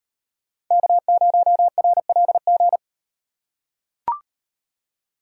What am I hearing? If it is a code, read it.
Morse code, heard as a keyed tone, about 38 wpm, 700 Hz. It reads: K0RLG